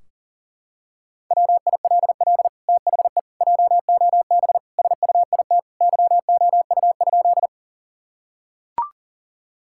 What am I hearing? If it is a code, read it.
Morse code, heard as a keyed tone, about 40 wpm, 700 Hz. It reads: WILL THE JOB SUIT YOU?
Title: Sentences from Top 1600 Words 40wpm